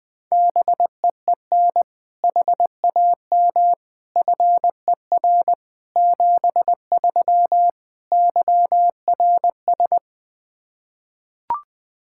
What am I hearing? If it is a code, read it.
Morse code, heard as a keyed tone, about 20 wpm, 700 Hz. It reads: BEEN HAM FER 73 YRS
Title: Practice QSO - Exchange 3 - More Info - 20wpm